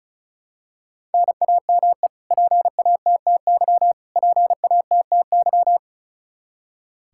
Morse code, heard as a keyed tone, about 36 wpm, 700 Hz. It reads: NAME PATTY PATTY
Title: Practice QSO - Exchange 1 - Beginning - 35wpm